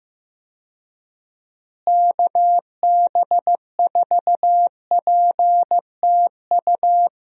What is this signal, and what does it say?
Morse code, heard as a keyed tone, about 15 wpm, 700 Hz. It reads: KB4PTU
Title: USA Call Signs - 6 Characters - Repeated 2x - 15wpm